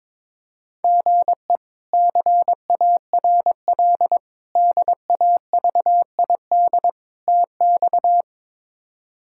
Morse code, heard as a keyed tone, about 22 wpm, 700 Hz. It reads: GE CARL DAVID TX
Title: K1USN SST QSO Practice - 22wpm